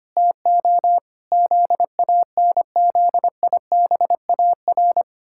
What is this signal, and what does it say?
Morse code, heard as a keyed tone, about 25 wpm, 700 Hz. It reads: TO ZANZIBAR
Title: Pangram Challenge v2 - 25wpm